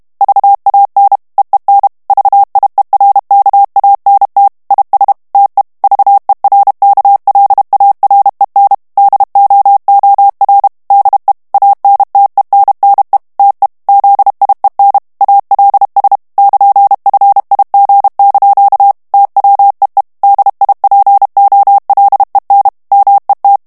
Morse code, heard as a keyed tone, about 32 wpm, 800 Hz. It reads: VAN EEN VIERKANT IS TE VERKLAREN DOOR DE ANTENNE TE ZIEN ALS <KN>FIG) TWEE DIPOLEN MET
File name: morse mp3 / HOLLANDAIS texte clair 1 vitesse 30 mots minute